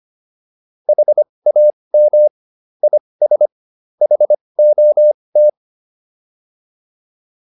Morse code, heard as a keyed tone, about 25 wpm, spaced as for 20 wpm, 600 Hz. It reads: HAM IS HOT